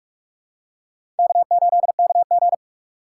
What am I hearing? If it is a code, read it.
Morse code, heard as a keyed tone, about 46 wpm, 700 Hz. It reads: K8KG
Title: USA Call Signs - 4 Characters - 4x Character Spacing - 45wpm